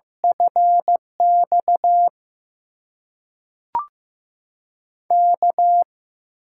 Morse code, heard as a keyed tone, about 15 wpm, 700 Hz. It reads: FX K
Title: USA Call Signs - 6 Characters - 6x Character Spacing - 15wpm